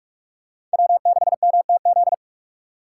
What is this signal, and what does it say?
Morse code, heard as a keyed tone, about 44 wpm, 700 Hz. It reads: W6MTZ